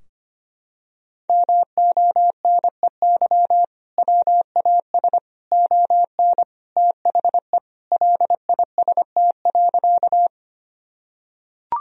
Morse code, heard as a keyed tone, about 25 wpm, 700 Hz. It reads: MONEY WAS ON THE LIST.
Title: Sentences from Top 1000 Words - Review Part 2 - 25wpm